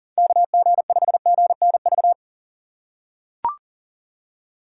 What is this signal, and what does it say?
Morse code, heard as a keyed tone, about 40 wpm, 700 Hz. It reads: KG5GNV
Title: USA Call Signs - 6 Characters - 4x Character Spacing - 40wpm